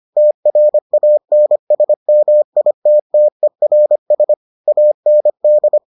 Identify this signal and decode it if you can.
Morse code, heard as a keyed tone, about 25 wpm, 600 Hz. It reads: TRANSMITTERS AND